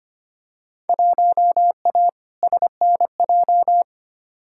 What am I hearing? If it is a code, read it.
Morse code, heard as a keyed tone, about 25 wpm, 700 Hz. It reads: 1A SNJ
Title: Field Day Exchanges - 25wpm